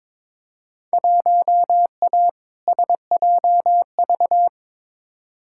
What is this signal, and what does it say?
Morse code, heard as a keyed tone, about 22 wpm, 700 Hz. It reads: 1A SJV
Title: Field Day Exchanges Encore - 22wpm